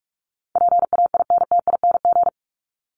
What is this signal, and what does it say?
Morse code, heard as a keyed tone, about 44 wpm, 700 Hz. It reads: PAINTING
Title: ING Suffix - Top 100 Words with Base Words 45wpm